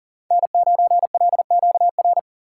Morse code, heard as a keyed tone, about 40 wpm, 700 Hz. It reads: N9LQR
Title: USA Call Signs - 5 Characters - 6x Character Spacing - 40wpm